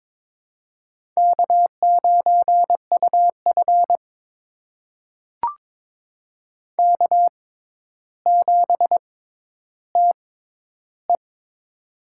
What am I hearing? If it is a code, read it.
Morse code, heard as a keyed tone, about 22 wpm, spaced as for 11 wpm, 700 Hz. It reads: K9UF K7TE
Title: USA Call Signs - 4 Characters - 6x Character Spacing - 22wpm